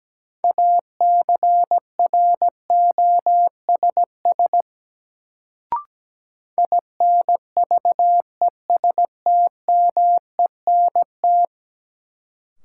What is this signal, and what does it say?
Morse code, heard as a keyed tone, about 17 wpm, 700 Hz. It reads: ACROSS INVESTMENT
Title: Sentences from Top 700 Words - New Word Drill - 17wpm